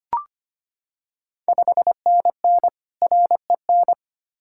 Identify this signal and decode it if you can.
Morse code, heard as a keyed tone, about 25 wpm, 700 Hz. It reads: 5NN REN
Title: Ontario QSO Party Exchange 25wpm